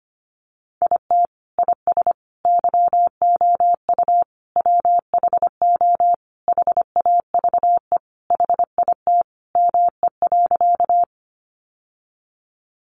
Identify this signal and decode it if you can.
Morse code, heard as a keyed tone, about 25 wpm, 700 Hz. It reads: IT IS YOU WHO HAVE HIT ME.
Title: Sentences from Top 1200 Words 25wpm